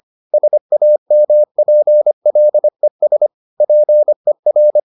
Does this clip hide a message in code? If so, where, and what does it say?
Yes: Morse code from 0.3 to 4.8 s, heard as a keyed tone, about 25 wpm, 600 Hz. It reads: SAMPLES PER